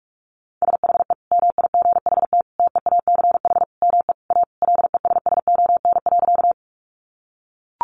Morse code, heard as a keyed tone, about 44 wpm, 700 Hz. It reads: SHE MIGHT TEACH ME A LESSON.
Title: Sentences from Top 1200 Words 45wpm